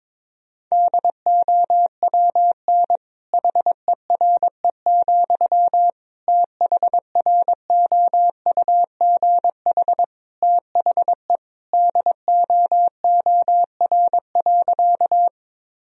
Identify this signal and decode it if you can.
Morse code, heard as a keyed tone, about 21 wpm, 700 Hz. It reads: DOWN HERE, THROUGH THE DOOR.
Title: Sentences from Top 1000 Words - Review Part 1 - 22wpm